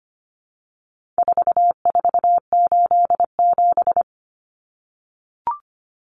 Morse code, heard as a keyed tone, about 25 wpm, 700 Hz. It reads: 4487
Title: Numbers - 4-Digit 25wpm